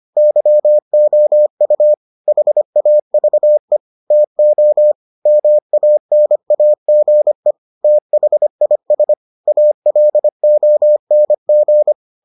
Morse code, heard as a keyed tone, about 25 wpm, 600 Hz. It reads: YOU HAVE TO MANAGE THIS ALONG